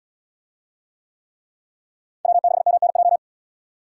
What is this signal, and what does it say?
Morse code, heard as a keyed tone, about 77 wpm, 700 Hz. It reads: W6MNJ